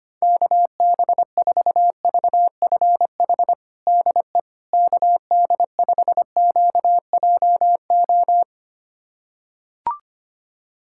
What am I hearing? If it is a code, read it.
Morse code, heard as a keyed tone, about 25 wpm, 700 Hz. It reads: KB4VFH DE KD5QJO
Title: USA Callsign DE Callsign 25wpm